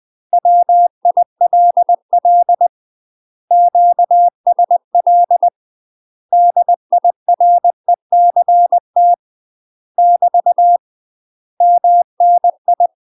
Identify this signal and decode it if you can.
Morse code, heard as a keyed tone, about 20 wpm, 700 Hz. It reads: WILL QSL DIRECT <BT> MNI
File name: UK MASTER QSOs TRAINING / 20WPM / UK MASTER QSOs 0069